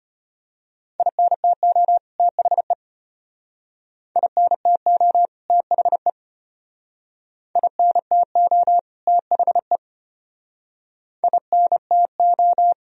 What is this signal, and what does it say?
Morse code, heard as a keyed tone, about 32 wpm, 700 Hz. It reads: INTO THE INTO THE INTO THE INTO